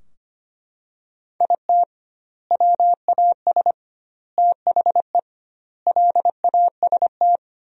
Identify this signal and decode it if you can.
Morse code, heard as a keyed tone, about 25 wpm, 700 Hz. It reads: IT WAS THE LAST
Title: 2x Word-Spacing - Sentences from Top 300 Words - 25wpm